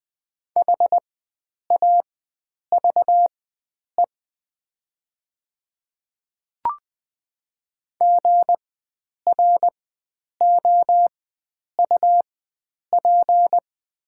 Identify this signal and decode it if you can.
Morse code, heard as a keyed tone, about 20 wpm, spaced as for 9 wpm, 700 Hz. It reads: HAVE GROUP
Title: 4x Character Spacing - Top 300 Words - 20wpm